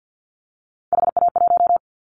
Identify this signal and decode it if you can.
Morse code, heard as a keyed tone, about 50 wpm, 700 Hz. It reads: HA1
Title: Callsign Prefixes for Top 50 DXCC - 50wpm